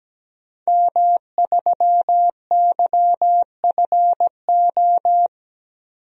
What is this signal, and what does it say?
Morse code, heard as a keyed tone, about 17 wpm, 700 Hz. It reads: M3YFO
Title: DXCC-100 Call Signs - 5 Characters - 17wpm